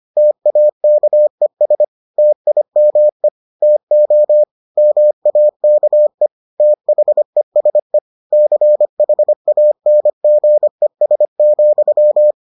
Morse code, heard as a keyed tone, about 25 wpm, 600 Hz. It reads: TAKES TIME TO MAKE THESE CHANGES,